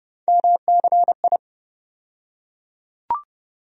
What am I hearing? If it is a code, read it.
Morse code, heard as a keyed tone, about 30 wpm, 700 Hz. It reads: MCI
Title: USA Call Sign Postfixes - 3 Letters 30wpm